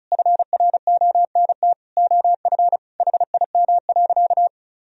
Morse code, heard as a keyed tone, about 36 wpm, 700 Hz. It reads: FRONT OF HIM.
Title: Sentences from Top 1000 Words - Review Part 1 - 35wpm